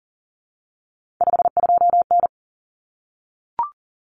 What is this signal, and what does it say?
Morse code, heard as a keyed tone, about 40 wpm, 700 Hz. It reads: 52N
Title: Common Cut Numbers - 3-Digit 40wpm